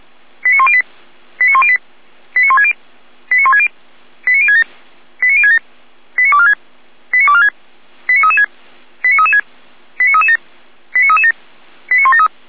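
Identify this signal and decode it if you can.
Five-tone selective call (ZVEI), heard as quick runs of five tones, tones of about 70 ms. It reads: 89108 89108 89160 89160 89067 89067 89246 89246 89336 89336 89337 89337 89173
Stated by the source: ZVEI five-tone selective call received on an SDR.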